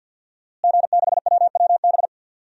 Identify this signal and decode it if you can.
Morse code, heard as a keyed tone, about 51 wpm, 700 Hz. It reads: G6WWB
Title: DXCC-100 Call Signs - 5 Characters - Repeated 2x - 50wpm